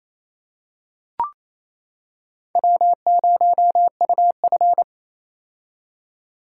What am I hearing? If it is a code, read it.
Morse code, heard as a keyed tone, about 28 wpm, 700 Hz. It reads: W0UF
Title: K1USN SST QSO Practice - 28wpm